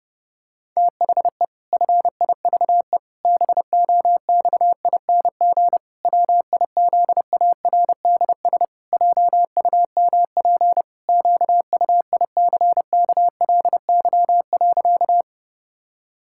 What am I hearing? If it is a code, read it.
Morse code, heard as a keyed tone, about 30 wpm, 700 Hz. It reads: THE FIVE BOXING WIZARDS JUMP QUICKLY.
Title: Pangram Challenge v2 - 30wpm